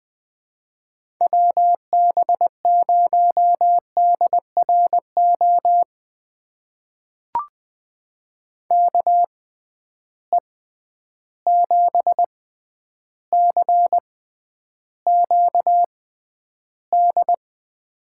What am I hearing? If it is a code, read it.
Morse code, heard as a keyed tone, about 20 wpm, spaced as for 10 wpm, 700 Hz. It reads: WB0DRO KE7CQD
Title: USA Call Signs - 6 Characters - 6x Character Spacing - 20wpm